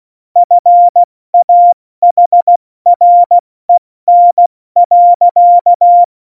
Morse code, heard as a keyed tone, about 16 wpm, 700 Hz. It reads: FAHREN.
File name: Morsekurs / Witze / Witz 19 Mai, 2020 70 F80